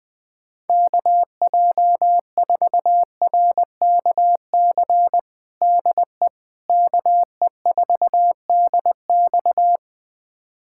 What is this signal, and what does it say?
Morse code, heard as a keyed tone, about 21 wpm, 700 Hz. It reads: KJ4RKC DE KE4DX
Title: USA Callsign DE Callsign 20wpm